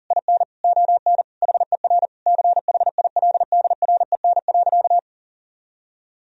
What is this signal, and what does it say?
Morse code, heard as a keyed tone, about 40 wpm, 700 Hz. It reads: IN ON HER CHILDREN.
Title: Sentences from Top 1000 Words - Review Part 3 - 40wpm